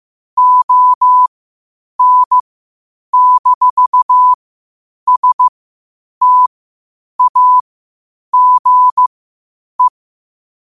Morse code, heard as a keyed tone, about 15 wpm, spaced as for 8 wpm, 1000 Hz. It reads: ON-STAGE
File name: qotdfiles / 8wpm